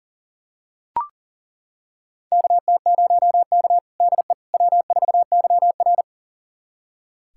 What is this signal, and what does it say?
Morse code, heard as a keyed tone, about 40 wpm, 690 Hz. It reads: KT0K DE W4YR
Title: USA Callsign DE Callsign - 4 Characters - 40wpm